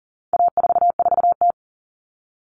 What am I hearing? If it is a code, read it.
Morse code, heard as a keyed tone, about 40 wpm, 700 Hz. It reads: A44T
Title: All Cut Numbers - 4-Digit 40wpm